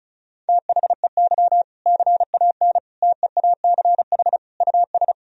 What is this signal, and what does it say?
Morse code, heard as a keyed tone, about 34 wpm, 700 Hz. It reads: THEY CAN TEACH US
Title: Sentences from Top 1000 Words - Review Part 2 - 35wpm